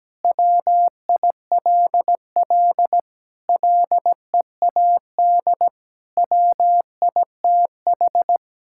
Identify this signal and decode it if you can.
Morse code, heard as a keyed tone, about 17 wpm, 700 Hz. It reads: WILL LEAD WITH
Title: Sentences from Top 1000 Words - Review Part 5 - 17wpm